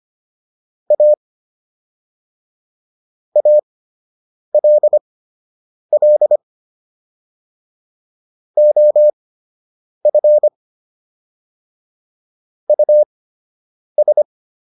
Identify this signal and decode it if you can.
Morse code, heard as a keyed tone, about 25 wpm, spaced as for 8 wpm, 600 Hz. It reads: A ALL OF US